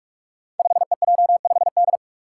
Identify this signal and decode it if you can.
Morse code, heard as a keyed tone, about 44 wpm, 700 Hz. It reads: 5EJ5D
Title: Five Challenging Letter-Number Combinations 45wpm